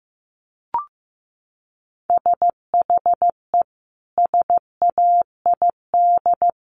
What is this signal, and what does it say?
Morse code, heard as a keyed tone, about 15 wpm, 700 Hz. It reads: SHE SAID
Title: Sentences from Top 1100 Words 15wpm